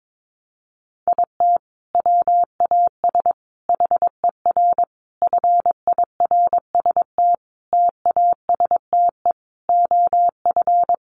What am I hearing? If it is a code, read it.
Morse code, heard as a keyed tone, about 22 wpm, 700 Hz. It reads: IT WAS HER FIRST TASTE OF
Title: Sentences from Top 1100 Words 22wpm